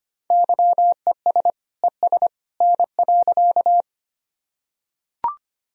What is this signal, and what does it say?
Morse code, heard as a keyed tone, about 25 wpm, 700 Hz. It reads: YES ES N.
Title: Sentences from Top 500 Words and Abbreviations 25wpm